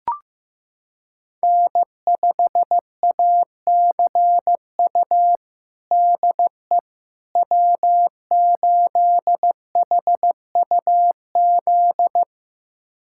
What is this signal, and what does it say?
Morse code, heard as a keyed tone, about 15 wpm, 700 Hz. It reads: N5ACU DE W8HUZ
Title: USA Callsign DE Callsign - 5 Characters - 15wpm